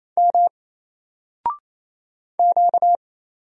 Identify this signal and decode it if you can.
Morse code, heard as a keyed tone, about 27 wpm, 700 Hz. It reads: M Q
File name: Single-Letter-Number-28wpm